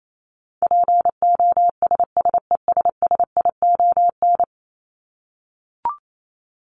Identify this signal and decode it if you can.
Morse code, heard as a keyed tone, about 28 wpm, 700 Hz. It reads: POSSESSION
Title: Top 1251-1500 Words 28wpm